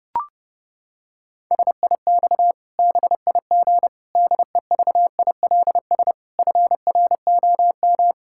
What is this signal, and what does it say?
Morse code, heard as a keyed tone, about 30 wpm, 700 Hz. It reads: SIX BIG DEVILS FROM
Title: Pangram Challenge v2 - 30wpm